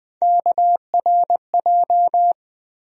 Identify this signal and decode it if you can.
Morse code, heard as a keyed tone, about 21 wpm, 700 Hz. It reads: KRJ
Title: USA Call Sign Postfixes - 3 Letters 20wpm